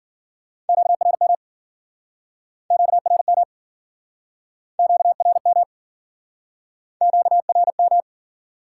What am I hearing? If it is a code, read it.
Morse code, heard as a keyed tone, about 55 wpm, 700 Hz. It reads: QRM QRM QRM QRM